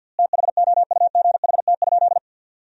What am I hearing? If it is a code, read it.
Morse code, heard as a keyed tone, about 47 wpm, 700 Hz. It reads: THOUGHT?
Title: 4x Word-Spacing - Sentences from Top 300 Words - 50wpm